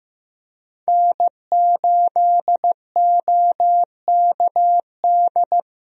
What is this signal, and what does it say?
Morse code, heard as a keyed tone, about 15 wpm, 700 Hz. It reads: N8OKD
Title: USA Call Signs - 5 Characters - Repeated 2x - 15wpm